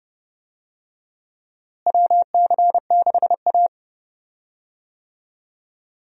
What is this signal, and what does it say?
Morse code, heard as a keyed tone, about 32 wpm, 700 Hz. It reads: WC6A
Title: USA Call Signs - 4 Characters - Repeated 3x - 30wpm